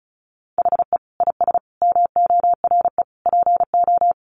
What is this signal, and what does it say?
Morse code, heard as a keyed tone, about 34 wpm, 700 Hz. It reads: HE IS MORE PO
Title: Sentences from Top 1300 Words 35wpm